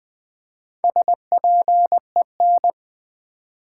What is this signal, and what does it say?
Morse code, heard as a keyed tone, about 20 wpm, 700 Hz. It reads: SPEN
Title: Build A Word - 4 Letters At A Time - 8-Letter Words - 20wpm